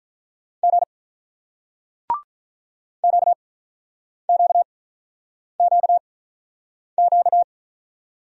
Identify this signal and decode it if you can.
Morse code, heard as a keyed tone, about 45 wpm, spaced as for 9 wpm, 700 Hz. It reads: G QQQQ